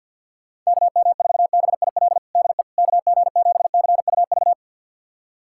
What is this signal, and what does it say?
Morse code, heard as a keyed tone, about 47 wpm, 700 Hz. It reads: KM4BIL DE KG7CUV
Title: USA Callsign DE Callsign - 6 Characters - 50wpm